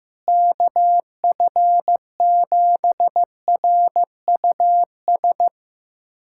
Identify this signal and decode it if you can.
Morse code, heard as a keyed tone, about 15 wpm, 700 Hz. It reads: KF7RUS
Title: USA Callsign DE Callsign - 6 Characters - 15wpm